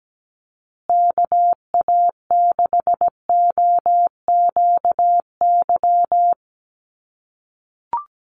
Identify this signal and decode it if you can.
Morse code, heard as a keyed tone, about 17 wpm, 700 Hz. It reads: KA6OQY
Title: USA Call Signs Repeated 2x 17wpm